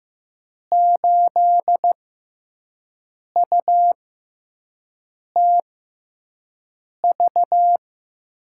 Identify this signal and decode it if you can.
Morse code, heard as a keyed tone, about 15 wpm, spaced as for 5 wpm, 700 Hz. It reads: 8UTV
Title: USA Call Signs - 6x Character Spacing - 15wpm